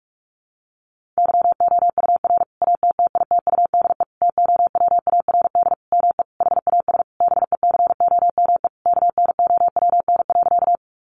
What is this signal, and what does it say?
Morse code, heard as a keyed tone, about 44 wpm, 700 Hz. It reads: YOUR ATTITUDE TOWARD ME HAS BECOME KNOWN.
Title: Sentences from Top 1300 Words 45wpm